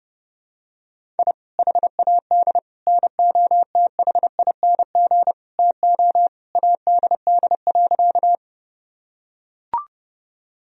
Morse code, heard as a keyed tone, about 31 wpm, 700 Hz. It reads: I HAD NOTHING TO ADD.
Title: Sentences from Top 1000 Words - Review Part 1 - 30wpm